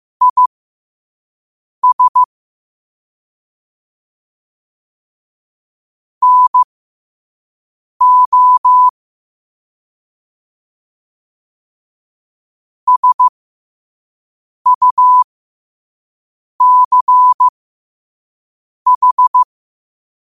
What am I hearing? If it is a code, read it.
Morse code, heard as a keyed tone, about 15 wpm, spaced as for 5 wpm, 1000 Hz. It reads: IS NO SUCH
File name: qotdfiles / 5wpm